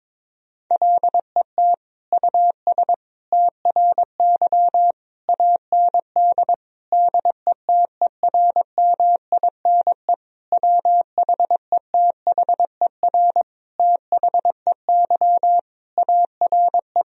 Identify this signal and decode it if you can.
Morse code, heard as a keyed tone, about 22 wpm, 700 Hz. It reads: LET US TRY AND DETERMINE WHETHER THEY ARE
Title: Sentences from Top 1000 Words - Review Part 4 - 22wpm